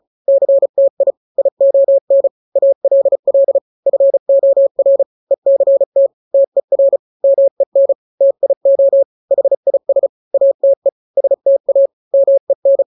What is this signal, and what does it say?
Morse code, heard as a keyed tone, about 34 wpm, 550 Hz. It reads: CTI ION ALL FOR ECT TER MEN TIO HIS ATE STA MEN
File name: 3-grams 35 30 WPM